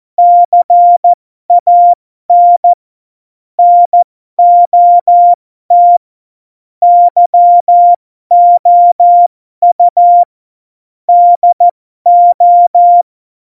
Morse code, heard as a keyed tone, about 14 wpm, spaced as for 12 wpm, 700 Hz. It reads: CAN NOT YOU DO
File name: Morsekurs / Witze